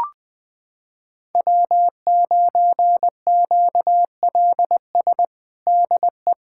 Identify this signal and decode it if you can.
Morse code, heard as a keyed tone, about 20 wpm, 700 Hz. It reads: W9QLS DE
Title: USA Callsign DE Callsign - 5 Characters - 20wpm